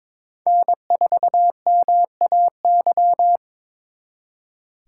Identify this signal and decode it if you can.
Morse code, heard as a keyed tone, about 26 wpm, 700 Hz. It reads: N4MAY
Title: Speed Racing - USA 1x3 Call Signs 22wpm